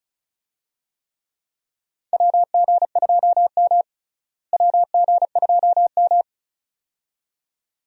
Morse code, heard as a keyed tone, about 34 wpm, 700 Hz. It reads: WG2M WG2M
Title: USA Call Signs - 4 Characters - Repeated 2x - 35wpm